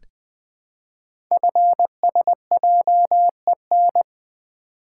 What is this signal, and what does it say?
Morse code, heard as a keyed tone, about 20 wpm, 700 Hz. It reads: FSJEN
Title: Code Groups - 5 Letters - 20wpm